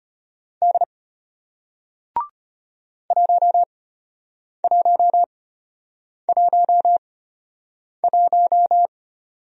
Morse code, heard as a keyed tone, about 33 wpm, spaced as for 8 wpm, 700 Hz. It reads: D 1111